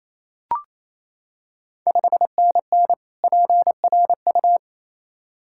Morse code, heard as a keyed tone, about 27 wpm, 700 Hz. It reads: 5NN PRU
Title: Ontario QSO Party Exchange 28wpm